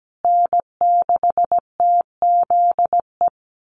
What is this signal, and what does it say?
Morse code, heard as a keyed tone, about 17 wpm, 700 Hz. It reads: N6TZE
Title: USA Call Signs Repeated 2x 17wpm